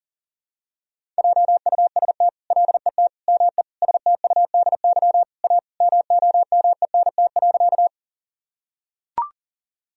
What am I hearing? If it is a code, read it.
Morse code, heard as a keyed tone, about 37 wpm, 700 Hz. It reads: JUST LET ME STUDY A MOMENT.
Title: Sentences from Top 1000 Words - Review Part 1 - 40wpm